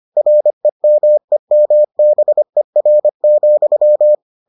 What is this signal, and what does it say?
Morse code, heard as a keyed tone, about 25 wpm, 600 Hz. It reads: REMEMBER,